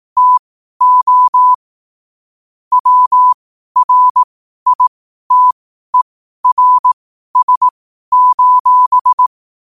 Morse code, heard as a keyed tone, about 18 wpm, spaced as for 12 wpm, 1000 Hz. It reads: TO WRITERS:
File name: qotdfiles / 12wpm